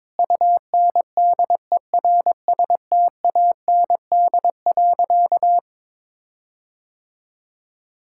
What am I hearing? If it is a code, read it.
Morse code, heard as a keyed tone, about 21 wpm, 700 Hz. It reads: UNDERSTAND.
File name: practice / 22wpm / Sentences with 4-Word Phrases - 22wpm